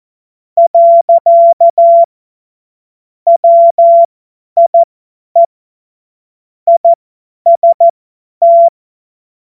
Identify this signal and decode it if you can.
Morse code, heard as a keyed tone, about 14 wpm, spaced as for 10 wpm, 680 Hz. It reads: . WIE IST
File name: Morsekurs / Witze / Witz 26. Oktober 2021, 50 F70